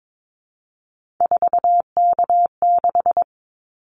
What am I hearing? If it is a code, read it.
Morse code, heard as a keyed tone, about 22 wpm, 700 Hz. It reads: 4K6
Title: Callsign Prefixes for Top 100 DXCC - 22wpm